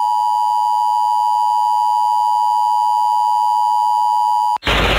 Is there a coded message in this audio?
Yes: Emergency Alert System (SAME) tones (the 853 and 960 Hz attention tone) from 0.0 to 4.6 s.